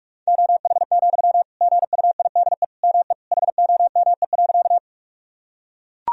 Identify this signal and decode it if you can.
Morse code, heard as a keyed tone, about 41 wpm, 700 Hz. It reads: OH, GUIDE ME HOME.
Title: Sentences from Top 1000 Words - Review Part 3 - 45wpm